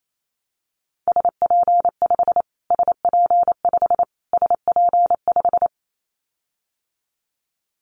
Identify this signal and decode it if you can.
Morse code, heard as a keyed tone, about 28 wpm, 700 Hz. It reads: SP5 SP5 SP5
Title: Callsign Prefixes for Top 25 DXCC - Repeated 3x - 28wpm